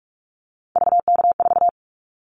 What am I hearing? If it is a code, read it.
Morse code, heard as a keyed tone, about 44 wpm, 700 Hz. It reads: VK4
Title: Callsign Prefixes for Top 50 DXCC - 45wpm